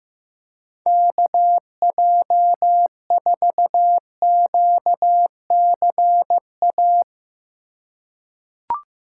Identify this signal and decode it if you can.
Morse code, heard as a keyed tone, about 15 wpm, 700 Hz. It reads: KJ4QCA
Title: USA Call Signs - 3x Character Spacing - 15wpm